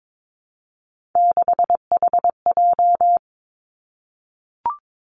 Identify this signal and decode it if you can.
Morse code, heard as a keyed tone, about 22 wpm, 700 Hz. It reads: 6HJ
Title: Three Challenging Letter-Number Combinations 22wpm